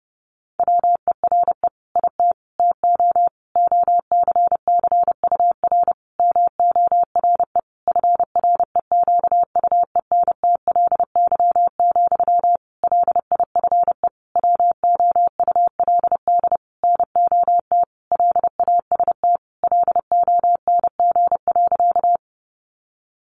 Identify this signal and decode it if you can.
Morse code, heard as a keyed tone, about 30 wpm, 700 Hz. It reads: WERE IT TO OCCUR MORE FREQUENTLY, LIFE WOULD NOT LAST LONG.
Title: Sentences from Top 1300 Words 30wpm